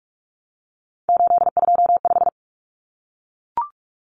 Morse code, heard as a keyed tone, about 44 wpm, 700 Hz. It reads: 825
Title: Common Cut Numbers - 3-Digit 45wpm